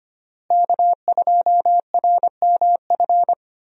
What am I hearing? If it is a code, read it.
Morse code, heard as a keyed tone, about 25 wpm, 700 Hz. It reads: K2RMF
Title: USA Call Signs - 5 Characters - Repeated 2x - 25wpm